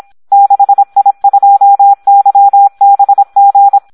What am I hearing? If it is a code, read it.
Morse code, heard as a keyed tone, about 26 wpm, 800 Hz. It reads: 6I2YBG
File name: Audio practica CW / SD 1 26